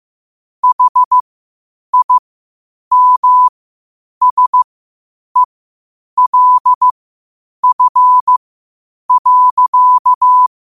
Morse code, heard as a keyed tone, about 15 wpm, spaced as for 8 wpm, 1000 Hz. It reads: HIMSELF.
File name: qotdfiles / 8wpm